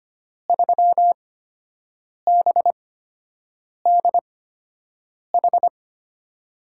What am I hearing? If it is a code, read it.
Morse code, heard as a keyed tone, about 25 wpm, spaced as for 7 wpm, 700 Hz. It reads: 3BDH
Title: USA Call Signs - 8x Character Spacing - 25wpm